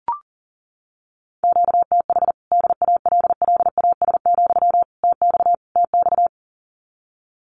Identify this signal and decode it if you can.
Morse code, heard as a keyed tone, about 40 wpm, 700 Hz. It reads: QTH DALLAS, TX TX
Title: Common QSO Phrases Encore 40wpm